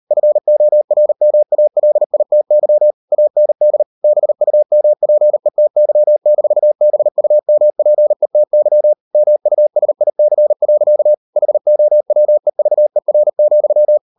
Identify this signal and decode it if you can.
Morse code, heard as a keyed tone, about 38 wpm, 600 Hz. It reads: FORMALITY AND BUMPETY-BUMPETY MUSIC. HOWEVER,